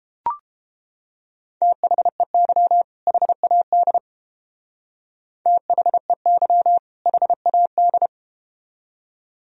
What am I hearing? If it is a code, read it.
Morse code, heard as a keyed tone, about 31 wpm, 700 Hz. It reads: THEY HAD THEY HAD